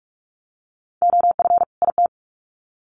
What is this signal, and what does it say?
Morse code, heard as a keyed tone, about 44 wpm, 700 Hz. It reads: OF IT
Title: Sets of 2 Words 45wpm